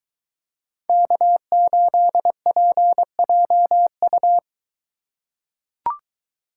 Morse code, heard as a keyed tone, about 23 wpm, 700 Hz. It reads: K8PJU